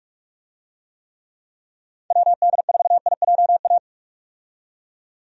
Morse code, heard as a keyed tone, about 46 wpm, 700 Hz. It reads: WD4IJA
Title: USA Call Signs - 6 Characters - 45wpm